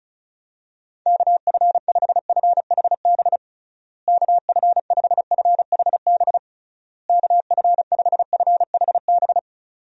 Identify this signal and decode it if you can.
Morse code, heard as a keyed tone, about 34 wpm, 700 Hz. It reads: KF5FHB KF5FHB KF5FHB
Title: USA Call Signs - 6 Characters - Repeated 3x - 35wpm